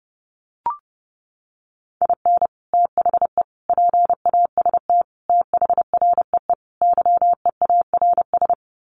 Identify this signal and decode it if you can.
Morse code, heard as a keyed tone, about 30 wpm, 700 Hz. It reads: IN THE PAST THREE YEARS
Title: Sets of 5 Words Encore2 30wpm